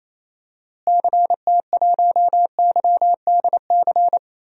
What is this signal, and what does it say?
Morse code, heard as a keyed tone, about 28 wpm, 700 Hz. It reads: CT1YDC
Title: DXCC-100 Call Signs - 6 Characters - Repeated 3x - 28wpm